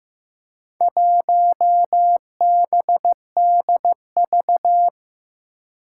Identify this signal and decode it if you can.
Morse code, heard as a keyed tone, about 15 wpm, 700 Hz. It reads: 1BDV
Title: Four Challenging Letter-Number Combinations 15wpm